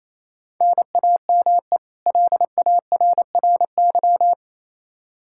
Morse code, heard as a keyed tone, about 27 wpm, 700 Hz. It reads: NAME LARRY
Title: Practice QSO - Exchange 1 - Beginning - 28wpm